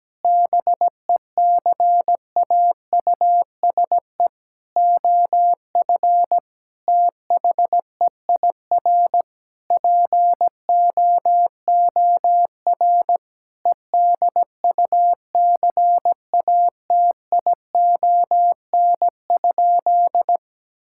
Morse code, heard as a keyed tone, about 17 wpm, 700 Hz. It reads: BECAUSE OF THEIR POOR EDUCATION?
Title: Sentences from Top 1000 Words - Review Part 6 - 17wpm